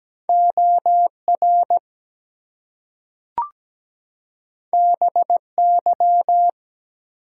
Morse code, heard as a keyed tone, about 17 wpm, 700 Hz. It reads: OR BY
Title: Two Letter Words - 17wpm